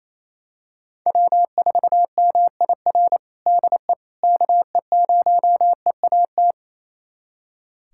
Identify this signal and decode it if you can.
Morse code, heard as a keyed tone, about 28 wpm, 700 Hz. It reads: W4MIR DE KE0EAT
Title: USA Callsign DE Callsign 28wpm